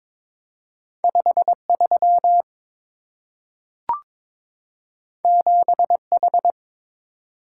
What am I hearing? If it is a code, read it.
Morse code, heard as a keyed tone, about 23 wpm, 700 Hz. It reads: 53 7H
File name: practice / 22wpm / Two Letter-Number Combinations - 22wpm